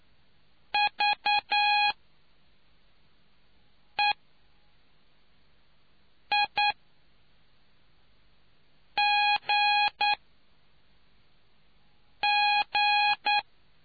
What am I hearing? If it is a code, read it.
Morse code, heard as a keyed tone, about 9 wpm, spaced as for 3 wpm, 830 Hz. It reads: VEIGG